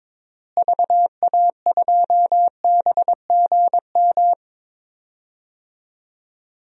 Morse code, heard as a keyed tone, about 22 wpm, 700 Hz. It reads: VA2BGM
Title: DXCC-100 Call Signs - 6 Characters - Repeated 2x - 22wpm